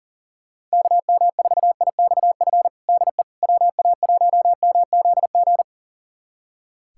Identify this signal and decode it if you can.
Morse code, heard as a keyed tone, about 40 wpm, 700 Hz. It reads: KM4IXF DE WA1MZG
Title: USA Callsign DE Callsign 40wpm